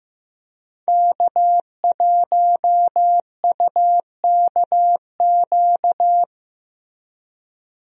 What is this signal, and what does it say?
Morse code, heard as a keyed tone, about 16 wpm, 700 Hz. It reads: K1UKQ
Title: USA Call Signs - 5 Characters - Repeated 2x - 15wpm